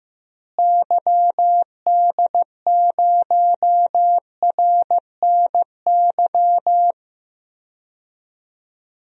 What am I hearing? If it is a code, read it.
Morse code, heard as a keyed tone, about 15 wpm, 700 Hz. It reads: YD0RNY
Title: DXCC-100 Call Signs - 6 Characters - Repeated 3x - 15wpm